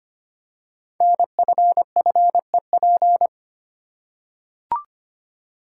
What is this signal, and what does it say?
Morse code, heard as a keyed tone, about 25 wpm, 700 Hz. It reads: NFFEP
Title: Code Groups - 5 Letters - 25wpm